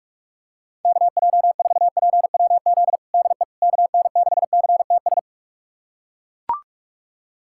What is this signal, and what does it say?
Morse code, heard as a keyed tone, about 41 wpm, 700 Hz. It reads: KJ4PWZ DE KN6CTS
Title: USA Callsign DE Callsign 45wpm